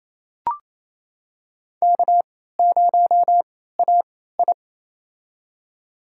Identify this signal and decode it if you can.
Morse code, heard as a keyed tone, about 28 wpm, spaced as for 16 wpm, 700 Hz. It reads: K0AI
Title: USA Call Signs - 3x Character Spacing - 28wpm